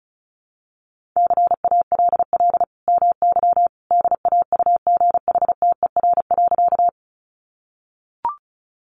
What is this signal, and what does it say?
Morse code, heard as a keyed tone, about 34 wpm, 700 Hz. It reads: CALL MY DAUGHTER.
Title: Sentences from Top 1500 Words 35wpm